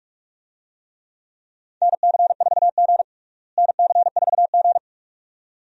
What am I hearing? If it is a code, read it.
Morse code, heard as a keyed tone, about 46 wpm, 700 Hz. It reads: NC4G NC4G
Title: USA Call Signs - 4 Characters - Repeated 2x - 45wpm